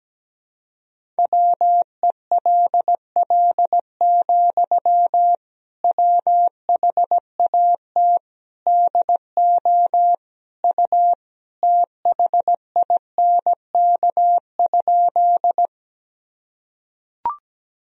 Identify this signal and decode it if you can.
Morse code, heard as a keyed tone, about 17 wpm, 700 Hz. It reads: WELL, WHAT DO U THINK?
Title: Sentences from Top 200 Words and Abbreviations 17wpm